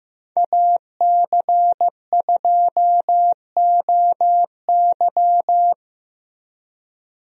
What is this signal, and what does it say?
Morse code, heard as a keyed tone, about 15 wpm, 700 Hz. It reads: AC2OY
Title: USA Call Signs - 5 Characters - Repeated 2x - 15wpm